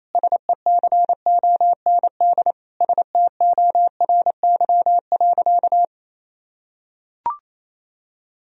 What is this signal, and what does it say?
Morse code, heard as a keyed tone, about 28 wpm, 700 Hz. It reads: SECOND STORY.
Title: Sentences from Top 1000 Words - Review Part 1 - 28wpm